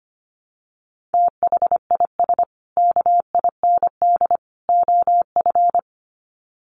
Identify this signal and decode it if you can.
Morse code, heard as a keyed tone, about 25 wpm, 700 Hz. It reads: THIS KIND OF
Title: Sets of 3 Words 25wpm